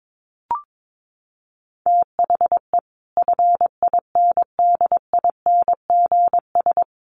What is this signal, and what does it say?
Morse code, heard as a keyed tone, about 22 wpm, 700 Hz. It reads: THE FINDINGS
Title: Sets of 5 Words Encore2 22wpm